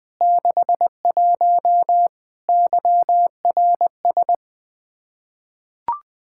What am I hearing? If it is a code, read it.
Morse code, heard as a keyed tone, about 20 wpm, 700 Hz. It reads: 61 YRS
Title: Practice QSO - Exchange 3 - More Info - 20wpm